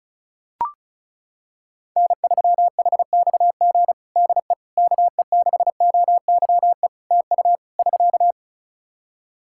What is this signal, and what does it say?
Morse code, heard as a keyed tone, about 36 wpm, 700 Hz. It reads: N3HXG DE KE6OYE TU <SK>
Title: Practice QSO - Exchange 4 - Ending - 35wpm